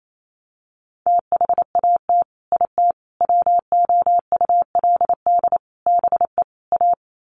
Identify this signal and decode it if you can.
Morse code, heard as a keyed tone, about 28 wpm, 700 Hz. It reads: THAT IT WOULD BE A
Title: Sets of 5 Words Encore2 28wpm